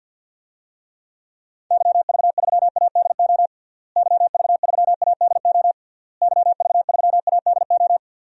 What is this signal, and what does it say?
Morse code, heard as a keyed tone, about 49 wpm, 700 Hz. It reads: YV3ADO YV3ADO YV3ADO
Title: DXCC-100 Call Signs - 6 Characters - Repeated 3x - 50wpm